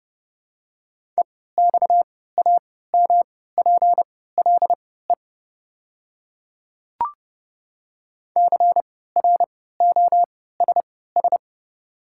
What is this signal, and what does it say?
Morse code, heard as a keyed tone, about 30 wpm, spaced as for 17 wpm, 700 Hz. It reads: EXAMPLE CROSS
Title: 3x Character Spacing - Top 500 Words - 30wpm